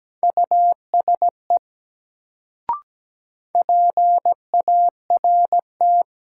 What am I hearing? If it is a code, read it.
Morse code, heard as a keyed tone, about 17 wpm, 700 Hz. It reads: USE PART
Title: Sets of 2 Words - New Word Drill - 17wpm